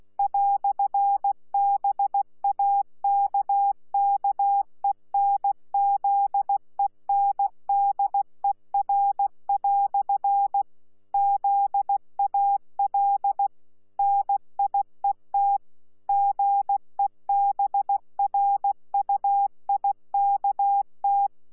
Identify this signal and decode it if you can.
Morse code, heard as a keyed tone, about 16 wpm, 800 Hz. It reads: "BAKKENZENDER" ZAL NIET GEBRUIKT
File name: Z PI4VRZ / PI4VRZ 14-jun-2025